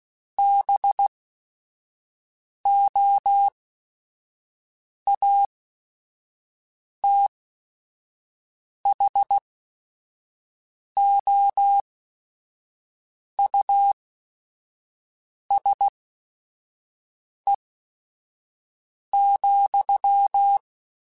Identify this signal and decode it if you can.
Morse code, heard as a keyed tone, about 16 wpm, spaced as for 5 wpm, 800 Hz. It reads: BOATHOUSE,
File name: General / General-mp3s / Houses @5